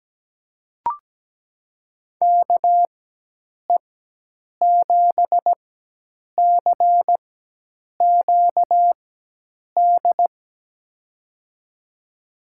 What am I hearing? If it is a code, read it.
Morse code, heard as a keyed tone, about 17 wpm, spaced as for 8 wpm, 700 Hz. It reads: KE7CQD
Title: USA Call Signs - 6 Characters - 4x Character Spacing - 17wpm